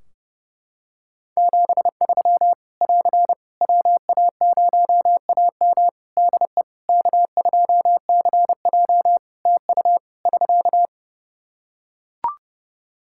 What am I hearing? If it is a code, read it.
Morse code, heard as a keyed tone, about 30 wpm, 700 Hz. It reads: 73 <AR> WA0AM DE K2CJ TU <SK>
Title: K1USN SST QSO Practice 30wpm